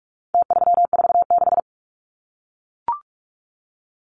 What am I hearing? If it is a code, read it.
Morse code, heard as a keyed tone, about 44 wpm, 700 Hz. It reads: T346
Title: Numbers - Contest Style - 4-Digit 45wpm